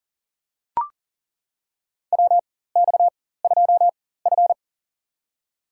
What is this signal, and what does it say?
Morse code, heard as a keyed tone, about 40 wpm, spaced as for 19 wpm, 700 Hz. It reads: WX2F
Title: USA Call Signs - 4 Characters - 4x Character Spacing - 40wpm